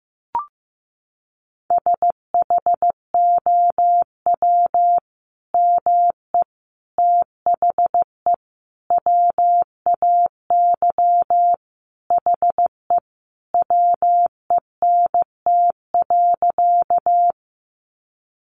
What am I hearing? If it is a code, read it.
Morse code, heard as a keyed tone, about 15 wpm, 700 Hz. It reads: SHOW ME THE WAY HE WENT.
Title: Sentences from Top 200 Words 15wpm